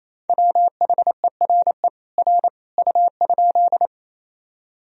Morse code, heard as a keyed tone, about 28 wpm, 700 Hz. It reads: WHERE R U?
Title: Sentences from Top 200 Words and Abbreviations 28wpm